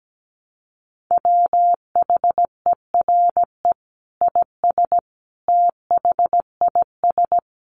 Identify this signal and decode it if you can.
Morse code, heard as a keyed tone, about 17 wpm, 700 Hz. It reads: WHERE IS THIS
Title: Sentences from Top 300 Words 17wpm